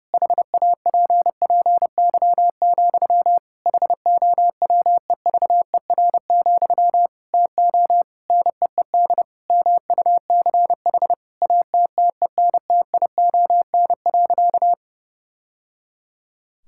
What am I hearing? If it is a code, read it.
Morse code, heard as a keyed tone, about 32 wpm, 700 Hz. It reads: HAPPY, HOWEVER, TO NEED MUCH ATTENTION.
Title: Sentences from Top 1000 Words - Review Part 6 - 30wpm